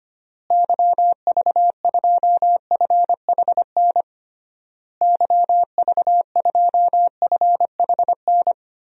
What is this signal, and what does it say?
Morse code, heard as a keyed tone, about 25 wpm, 700 Hz. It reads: YV2FHN YV2FHN
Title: DXCC-100 Call Signs - 6 Characters - Repeated 3x - 25wpm